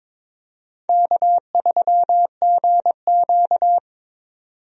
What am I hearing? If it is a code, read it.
Morse code, heard as a keyed tone, about 22 wpm, 700 Hz. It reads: K3GQ
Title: USA Call Signs - 4 Characters - 22wpm